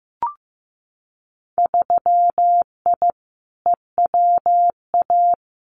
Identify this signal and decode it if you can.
Morse code, heard as a keyed tone, about 15 wpm, 700 Hz. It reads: 3I EWA
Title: Winter Field Day Exchanges Encore 15wpm